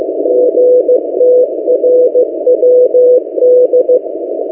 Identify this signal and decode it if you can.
Morse code, heard as a keyed tone, about 15 wpm, 500 Hz. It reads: GTRWD
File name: M08a- -1355utc-8096khz